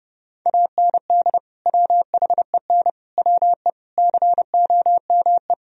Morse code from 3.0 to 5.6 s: WE COME

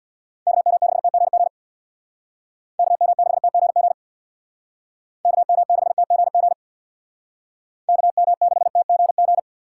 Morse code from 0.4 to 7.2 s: KM6TGG KM6TGG KM6TGG

